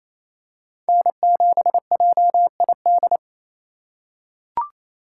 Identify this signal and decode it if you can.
Morse code, heard as a keyed tone, about 27 wpm, 700 Hz. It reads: N7JID
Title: USA Call Signs - 2x Character Spacing - 28wpm